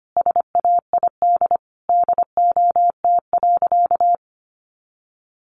Morse code, heard as a keyed tone, about 25 wpm, 700 Hz. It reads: SAID DOT.
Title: Sentences from Top 1500 Words 25wpm